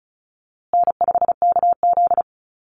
Morse code, heard as a keyed tone, about 34 wpm, 700 Hz. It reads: N5KZ
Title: USA Call Signs Repeated 3x 35wpm